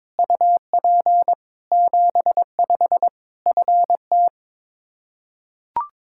Morse code, heard as a keyed tone, about 22 wpm, 700 Hz. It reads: UP 75 FT
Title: Practice QSO - Exchange 2 - Info - 22wpm